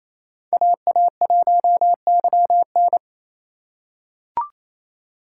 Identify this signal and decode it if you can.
Morse code, heard as a keyed tone, about 28 wpm, 700 Hz. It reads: AA1YN
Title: USA Call Signs - 5 Characters - 2x Character Spacing - 28wpm